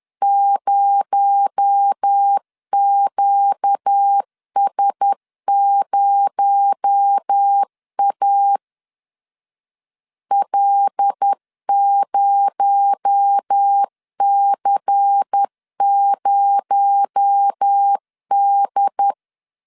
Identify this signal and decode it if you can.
Morse code, heard as a keyed tone, about 10 wpm, 800 Hz. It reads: 0QS0A L0C0D
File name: LEZIONI / 6 PKX0 / 6 PKX0 10wpm 2707